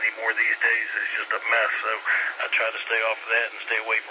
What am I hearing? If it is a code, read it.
Emergency Alert System (SAME) tones, heard as harsh buzzing bursts, with speech. It is heard from 0.8 to 3.6 s.